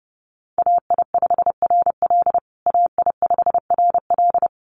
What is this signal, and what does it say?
Morse code, heard as a keyed tone, about 30 wpm, 700 Hz. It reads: AI5RL AI5RL
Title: USA Call Signs Repeated 2x 30wpm